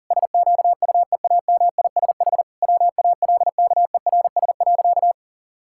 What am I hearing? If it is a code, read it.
Morse code, heard as a keyed tone, about 37 wpm, 700 Hz. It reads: SQUEAMISH WALKERS.
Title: Pangram Challenge v2 - 40wpm